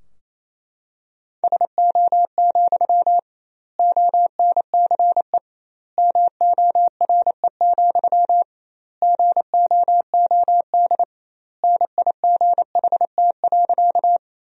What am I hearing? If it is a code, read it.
Morse code, heard as a keyed tone, about 28 wpm, 700 Hz. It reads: SO, ONCE MORE, GOOD NIGHT.
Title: 2x Word-Spacing - Sentences from Top 300 Words - 28wpm